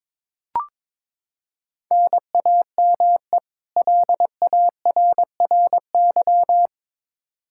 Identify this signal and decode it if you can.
Morse code, heard as a keyed tone, about 22 wpm, 700 Hz. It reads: NAME LARRY
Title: Practice QSO - Exchange 1 - Beginning - 22wpm